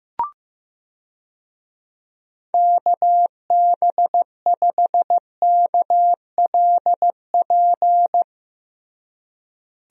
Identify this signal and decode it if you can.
Morse code, heard as a keyed tone, about 15 wpm, 700 Hz. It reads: KB5KLP
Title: USA Call Signs - 6 Characters - Repeated 2x - 15wpm